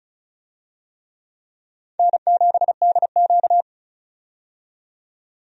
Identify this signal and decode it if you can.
Morse code, heard as a keyed tone, about 36 wpm, 700 Hz. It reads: N7DQ
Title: USA Call Signs - 4 Characters - 35wpm